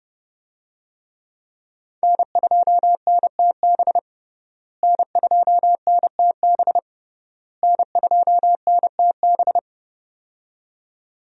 Morse code, heard as a keyed tone, about 30 wpm, 700 Hz. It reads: N2NTB N2NTB N2NTB
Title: USA Call Signs - 5 Characters - Repeated 3x - 30wpm